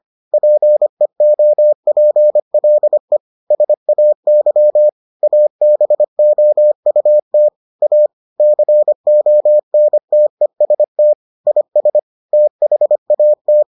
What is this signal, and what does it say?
Morse code, heard as a keyed tone, about 25 wpm, 600 Hz. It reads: PEOPLE SAY ABOUT A CONTEST IS THAT